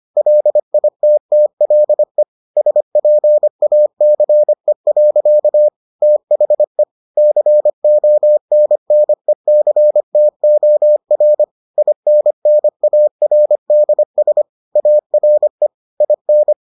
Morse code, heard as a keyed tone, about 25 wpm, 600 Hz. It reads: LITTLE SPACE. THE CONNECTOR INNARDS ARE IN